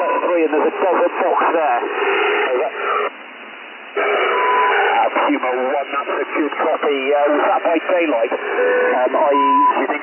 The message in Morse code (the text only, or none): none